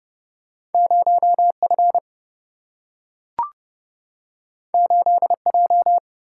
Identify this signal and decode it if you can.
Morse code, heard as a keyed tone, about 30 wpm, 700 Hz. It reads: 0 F 8 J
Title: Single-Letter Single-Number 30wpm